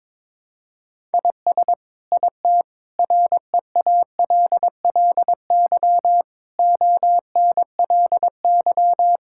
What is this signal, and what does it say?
Morse code, heard as a keyed tone, about 22 wpm, 700 Hz. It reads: IS IT REALLY ONLY
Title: Sentences from Top 1000 Words - Review Part 6 - 22wpm